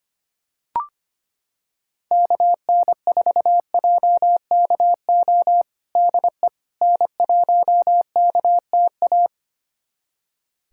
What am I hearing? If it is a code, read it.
Morse code, heard as a keyed tone, about 25 wpm, 700 Hz. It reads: KN4JKO DE N1KTA
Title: USA Callsign DE Callsign 25wpm